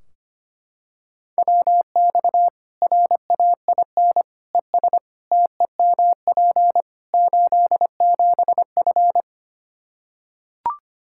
Morse code, heard as a keyed tone, about 25 wpm, 700 Hz. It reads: WX RAIN ES TEMP 87F
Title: Practice QSO - Exchange 2 - Info - 25wpm